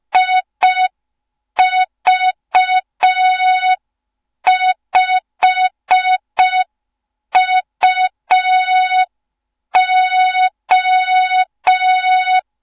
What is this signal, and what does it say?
Morse code, heard as a keyed tone, about 5 wpm, 3700 Hz. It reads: IV5UO
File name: Morze-HA2MN / hang / 07 GYAKORLAT